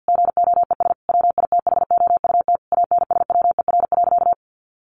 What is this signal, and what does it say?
Morse code, heard as a keyed tone, about 50 wpm, 700 Hz. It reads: GOES WITHOUT ANSWER.
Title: Sentences from Top 1500 Words 50wpm